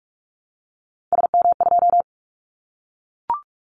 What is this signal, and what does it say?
Morse code, heard as a keyed tone, about 44 wpm, 700 Hz. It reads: SM2
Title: Callsign Prefixes for Top 25 DXCC - 45wpm